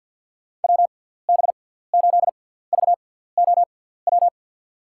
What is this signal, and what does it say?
Morse code, heard as a keyed tone, about 52 wpm, spaced as for 17 wpm, 700 Hz. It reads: WB8VOW
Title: USA Call Signs - 6 Characters - 6x Character Spacing - 50wpm